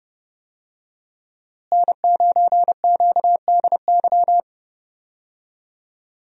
Morse code, heard as a keyed tone, about 32 wpm, 700 Hz. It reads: N9QDY
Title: USA Call Signs - 5 Characters - Repeated 3x - 30wpm